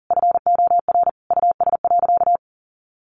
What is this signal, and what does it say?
Morse code, heard as a keyed tone, about 40 wpm, 700 Hz. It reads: FOR US.
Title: Sentences from Top 1200 Words 40wpm